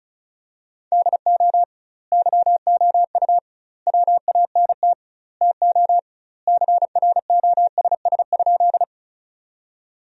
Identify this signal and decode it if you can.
Morse code, heard as a keyed tone, about 36 wpm, 700 Hz. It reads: DO YOU WANT TO CROSS?
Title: 2x Word-Spacing - Sentences from Top 300 Words - 35wpm